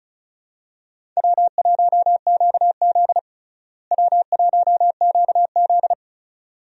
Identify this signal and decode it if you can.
Morse code, heard as a keyed tone, about 36 wpm, 700 Hz. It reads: W1QZ W1QZ
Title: USA Call Signs - 4 Characters - Repeated 2x - 35wpm